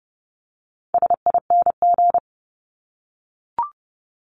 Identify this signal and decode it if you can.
Morse code, heard as a keyed tone, about 30 wpm, 700 Hz. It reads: SING
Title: Top 500 Words Repeated 30wpm